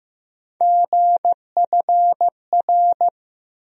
Morse code, heard as a keyed tone, about 15 wpm, 700 Hz. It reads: GFR
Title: USA Call Sign Postfixes - 3 Letters 15wpm